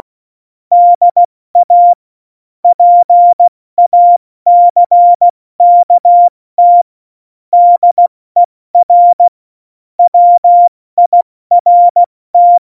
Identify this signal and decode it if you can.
Morse code, heard as a keyed tone, about 16 wpm, 700 Hz. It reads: DA PACKT DER WIRT
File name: Morsekurs / Witze / Witz 22. Dezember 2020, 70 F80